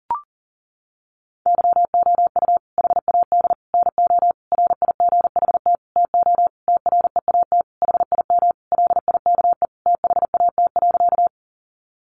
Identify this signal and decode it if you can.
Morse code, heard as a keyed tone, about 40 wpm, 700 Hz. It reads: YOU HAD NO RIGHT TO TREAT HIM LIKE THAT.
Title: Sentences from Top 1300 Words 40wpm